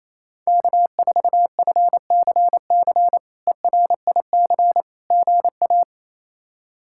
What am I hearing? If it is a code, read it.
Morse code, heard as a keyed tone, about 28 wpm, 700 Hz. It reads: K4FCC ERIC GA
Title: K1USN SST QSO Practice - 28wpm